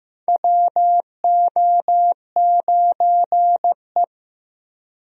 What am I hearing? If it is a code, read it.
Morse code, heard as a keyed tone, about 15 wpm, 700 Hz. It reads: WO9E
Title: USA Call Signs - 4 Characters - Repeated 2x - 15wpm